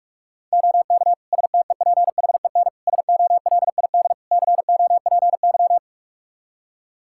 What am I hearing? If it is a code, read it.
Morse code, heard as a keyed tone, about 45 wpm, 700 Hz. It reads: OK STEPHEN SOLID COPY
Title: K1USN SST QSO Practice 45wpm